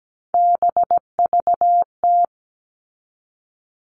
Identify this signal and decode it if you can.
Morse code, heard as a keyed tone, about 17 wpm, 700 Hz. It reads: BVT
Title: All Cut Numbers - 3-Digit 17wpm